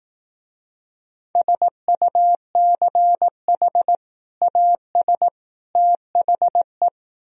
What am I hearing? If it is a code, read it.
Morse code, heard as a keyed tone, about 18 wpm, 700 Hz. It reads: SUCH AS THE
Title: Speed-Racing - Sets of 3 Words 15wpm